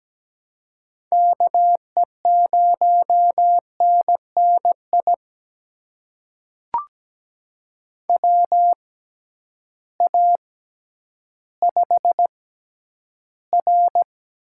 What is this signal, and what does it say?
Morse code, heard as a keyed tone, about 17 wpm, spaced as for 10 wpm, 700 Hz. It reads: KE0NNI WA5R
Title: USA Call Signs - 6 Characters - 6x Character Spacing - 17wpm